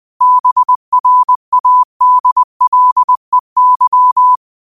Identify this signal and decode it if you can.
Morse code, heard as a keyed tone, about 20 wpm, 1000 Hz. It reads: BRADLEY
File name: qotdfiles / 20wpm